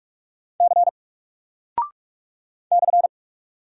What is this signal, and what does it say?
Morse code, heard as a keyed tone, about 44 wpm, 700 Hz. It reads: C /
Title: Single Character (Letter:Number:Punctuation:Prosign) 45wpm